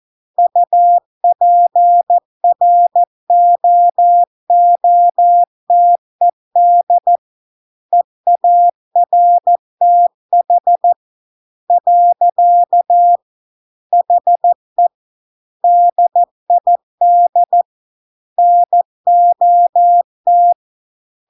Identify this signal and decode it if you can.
Morse code, heard as a keyed tone, about 14 wpm, 700 Hz. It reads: UPROOTED EARTH . HE DID NOT